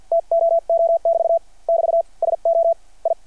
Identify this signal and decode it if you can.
Morse code, heard as a keyed tone, about 52 wpm, spaced as for 44 wpm, 660 Hz. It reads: TOO<BT> <BT> SO I
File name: archived / CRSnet